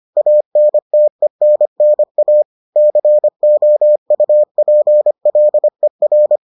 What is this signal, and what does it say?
Morse code, heard as a keyed tone, about 25 wpm, 600 Hz. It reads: ANTENNA COUPLER